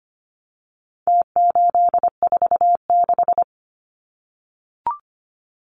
Morse code, heard as a keyed tone, about 25 wpm, 700 Hz. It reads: T846
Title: Numbers - Contest Style - 4-Digit 25wpm